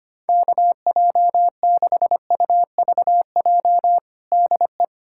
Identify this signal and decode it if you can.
Morse code, heard as a keyed tone, about 25 wpm, 700 Hz. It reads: KJ6UVJ DE
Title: USA Callsign DE Callsign - 6 Characters - 25wpm